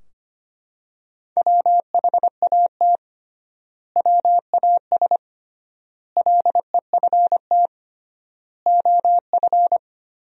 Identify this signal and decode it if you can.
Morse code, heard as a keyed tone, about 25 wpm, 700 Hz. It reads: WHAT WAS LEFT OF
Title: 3x Word-Spacing - Sentences from Top 300 Words - 25wpm